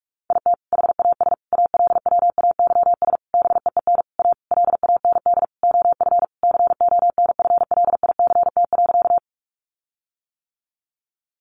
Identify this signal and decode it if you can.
Morse code, heard as a keyed tone, about 44 wpm, 700 Hz. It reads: IT HAS ALWAYS BEEN A LAND OF CONFLICT.
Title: Sentences from Top 1300 Words 45wpm